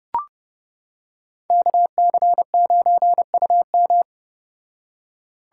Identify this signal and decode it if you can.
Morse code, heard as a keyed tone, about 29 wpm, 700 Hz. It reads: KC9UM